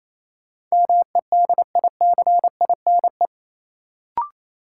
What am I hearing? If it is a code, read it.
Morse code, heard as a keyed tone, about 28 wpm, 700 Hz. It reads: MEDICINE
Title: Sentences from Top 900 Words - New Word Drill - 28wpm